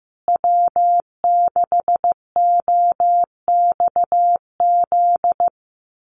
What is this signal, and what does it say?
Morse code, heard as a keyed tone, about 15 wpm, 700 Hz. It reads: W6OXZ
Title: USA Call Signs Repeated 2x 15wpm